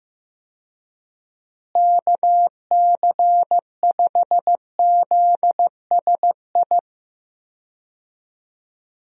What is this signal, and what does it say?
Morse code, heard as a keyed tone, about 15 wpm, 700 Hz. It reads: KC5ZSI
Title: USA Call Signs - 6 Characters - Repeated 2x - 15wpm